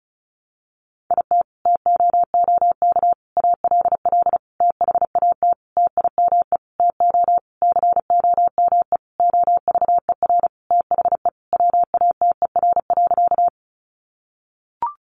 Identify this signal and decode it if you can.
Morse code, heard as a keyed tone, about 34 wpm, 700 Hz. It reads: IT TOOK ALL THAT TIME TO COME OVER THE WATER.
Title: Sentences from Top 300 Words 35wpm